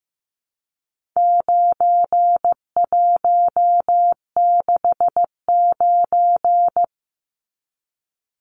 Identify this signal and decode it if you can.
Morse code, heard as a keyed tone, about 15 wpm, 700 Hz. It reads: 9169
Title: Numbers - 4-Digit 15wpm